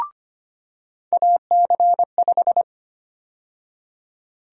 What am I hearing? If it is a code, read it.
Morse code, heard as a keyed tone, about 25 wpm, 700 Hz. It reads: AC5
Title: USA Call Sign Prefixes - 25wpm